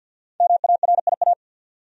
Morse code, heard as a keyed tone, about 48 wpm, 700 Hz. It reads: MARIA
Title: USA Common Names - Max 5 Letters - 50wpm